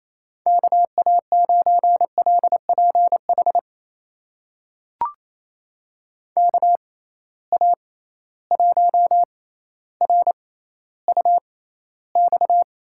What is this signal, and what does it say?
Morse code, heard as a keyed tone, about 28 wpm, spaced as for 14 wpm, 700 Hz. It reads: KA9LPH KA1RUX
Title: USA Call Signs - 6 Characters - 6x Character Spacing - 28wpm